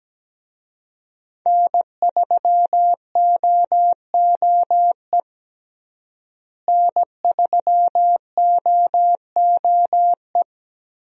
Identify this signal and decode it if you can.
Morse code, heard as a keyed tone, about 17 wpm, 700 Hz. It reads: N3OOE N3OOE
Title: USA Call Signs - 5 Characters - Repeated 2x - 17wpm